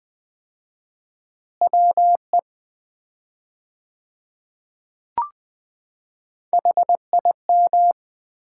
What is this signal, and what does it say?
Morse code, heard as a keyed tone, about 20 wpm, 700 Hz. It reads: WE HIM
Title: 4x Word-Spacing - Sets of 2 Words - 20wpm